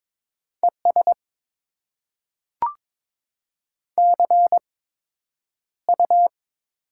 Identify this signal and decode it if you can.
Morse code, heard as a keyed tone, about 22 wpm, spaced as for 9 wpm, 700 Hz. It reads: ES CU
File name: Top 20 QSO Elements - 8x Character Spacing - 22wpm